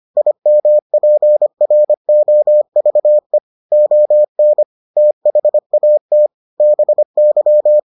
Morse code, heard as a keyed tone, about 25 wpm, 600 Hz. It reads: IMPROVE ON THAT BY